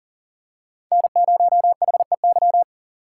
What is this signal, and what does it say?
Morse code, heard as a keyed tone, about 39 wpm, 700 Hz. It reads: N0HEY
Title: USA Call Signs - 5 Characters - 6x Character Spacing - 40wpm